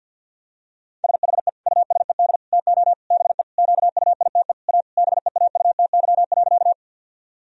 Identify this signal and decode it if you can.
Morse code, heard as a keyed tone, about 52 wpm, 700 Hz. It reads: SHE USED TO BE QUITE A BEAUTY.
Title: Sentences from Top 1000 Words - Review Part 2 - 50wpm